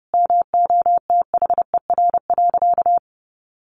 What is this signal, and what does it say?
Morse code, heard as a keyed tone, about 30 wpm, 700 Hz. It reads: MOTHER.
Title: Sentences from Top 1300 Words 30wpm